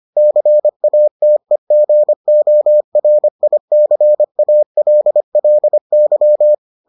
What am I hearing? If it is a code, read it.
Morse code, heard as a keyed tone, about 25 wpm, 600 Hz. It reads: CATEGORICALLY